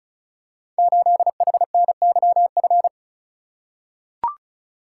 Morse code, heard as a keyed tone, about 36 wpm, 700 Hz. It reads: 8HNYF
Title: Code Groups - 5 Letter:Numbers - 35wpm